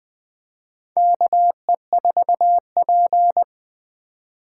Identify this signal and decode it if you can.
Morse code, heard as a keyed tone, about 20 wpm, 700 Hz. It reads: KE4P
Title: USA Call Signs - 4 Characters - 4x Character Spacing - 20wpm